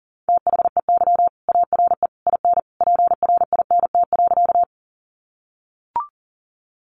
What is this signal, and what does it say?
Morse code, heard as a keyed tone, about 40 wpm, 700 Hz. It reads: THEY ARE IN PRINT.
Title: Sentences from Top 1400 Words 40wpm